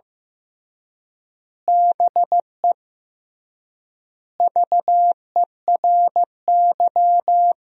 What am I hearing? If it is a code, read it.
Morse code, heard as a keyed tone, about 15 wpm, 700 Hz. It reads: BE VERY
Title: 3x Word-Spacing - Sentences from Top 200 Words - 15wpm